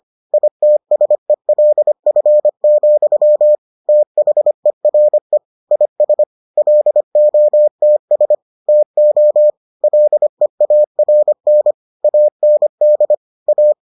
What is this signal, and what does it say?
Morse code, heard as a keyed tone, about 25 wpm, 600 Hz. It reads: ITSELF, THERE IS LOTS TO LEARN AND A